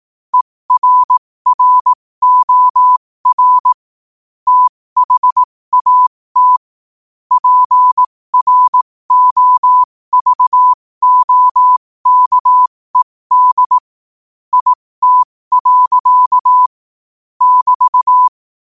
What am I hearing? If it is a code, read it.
Morse code, heard as a keyed tone, about 18 wpm, spaced as for 15 wpm, 1000 Hz. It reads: ERROR THAT PROVOKED IT. <BT>
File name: qotdfiles / 15wpm